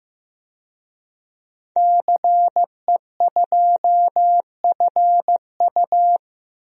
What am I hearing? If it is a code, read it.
Morse code, heard as a keyed tone, about 15 wpm, 700 Hz. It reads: CE2FU
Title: DXCC-100 Call Signs - 5 Characters - 15wpm